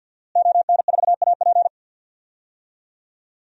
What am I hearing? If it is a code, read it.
Morse code, heard as a keyed tone, about 51 wpm, 700 Hz. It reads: ON4AP
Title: DXCC-100 Call Signs - 5 Characters - Repeated 2x - 50wpm